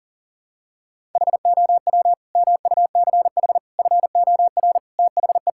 Morse code, heard as a keyed tone, about 40 wpm, 700 Hz. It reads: HOW MUCH FOR THE
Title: Sentences from Top 1000 Words - Review Part 5 - 40wpm